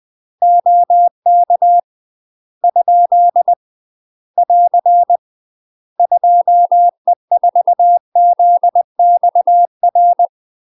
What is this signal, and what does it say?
Morse code, heard as a keyed tone, about 20 wpm, 700 Hz. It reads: OK ? <AR> 2E4ZXR
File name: UK MASTER QSOs TRAINING / 20WPM / UK MASTER QSOs 0063